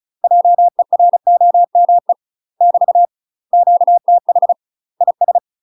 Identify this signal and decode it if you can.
Morse code, heard as a keyed tone, about 36 wpm, 700 Hz. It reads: JEROME <BT> QTH IS
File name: UK MASTER QSOs TRAINING / 35WPM / UK MASTER QSOs 0041